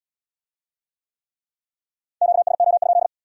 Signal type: Morse code

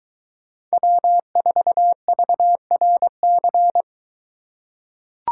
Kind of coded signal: Morse code

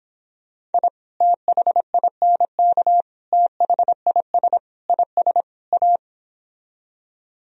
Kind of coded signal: Morse code

